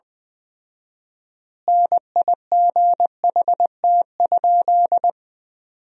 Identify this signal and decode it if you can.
Morse code, heard as a keyed tone, about 20 wpm, 700 Hz. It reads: NIGHT?
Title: 4x Word-Spacing - Sentences from Top 300 Words - 20wpm